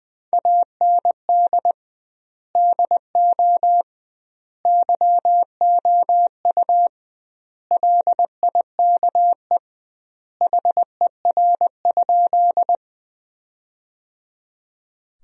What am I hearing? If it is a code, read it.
Morse code, heard as a keyed tone, about 20 wpm, 700 Hz. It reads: AND DO YOU LIKE HER?
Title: 2x Word-Spacing - Sentences from Top 100 Words - 20wpm